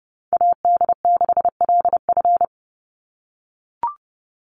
Morse code, heard as a keyed tone, about 30 wpm, 700 Hz. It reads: AD6LF
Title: USA Call Signs 30wpm